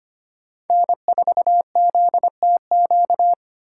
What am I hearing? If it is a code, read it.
Morse code, heard as a keyed tone, about 32 wpm, 700 Hz. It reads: N4ZTQ